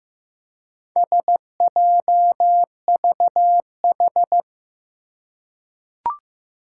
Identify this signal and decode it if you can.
Morse code, heard as a keyed tone, about 15 wpm, 700 Hz. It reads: SJVH
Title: Four Challenging Letter-Number Combinations 15wpm